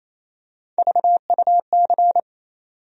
Morse code, heard as a keyed tone, about 28 wpm, 700 Hz. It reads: VUC
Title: USA Call Sign Postfixes - 3 Letters 28wpm